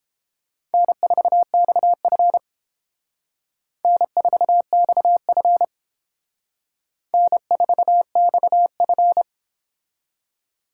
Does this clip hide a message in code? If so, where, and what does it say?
Yes: Morse code from 0.7 to 9.2 s, heard as a keyed tone, about 28 wpm, 700 Hz. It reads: N4XF N4XF N4XF